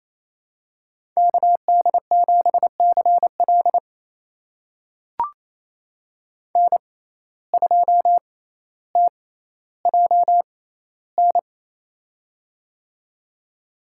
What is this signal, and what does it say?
Morse code, heard as a keyed tone, about 28 wpm, spaced as for 14 wpm, 700 Hz. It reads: KD7CL N2TJN
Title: USA Call Signs - 5 Characters - 6x Character Spacing - 28wpm